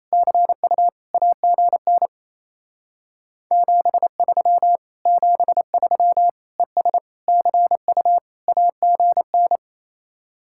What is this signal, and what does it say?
Morse code, heard as a keyed tone, about 35 wpm, 700 Hz. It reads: CU AGN 73 73 ES CU AGN